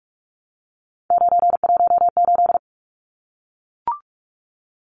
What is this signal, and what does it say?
Morse code, heard as a keyed tone, about 44 wpm, 700 Hz. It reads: 918
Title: Numbers - 3-Digit 45wpm